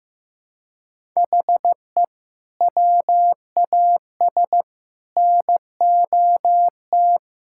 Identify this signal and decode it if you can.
Morse code, heard as a keyed tone, about 15 wpm, 700 Hz. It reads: HE WAS NOT
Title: Sentences from Top 1000 Words - Review Part 5 - 15wpm